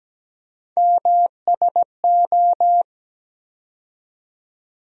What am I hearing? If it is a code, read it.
Morse code, heard as a keyed tone, about 17 wpm, 700 Hz. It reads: MSO
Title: USA Call Sign Postfixes - 3 Letters 17wpm